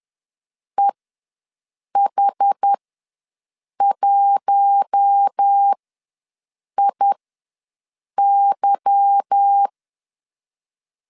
Morse code, heard as a keyed tone, about 10 wpm, spaced as for 6 wpm, 800 Hz. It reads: EH1IY
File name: LEZIONI / 10 SIGNS / 10C DOT ws 10wpm 4213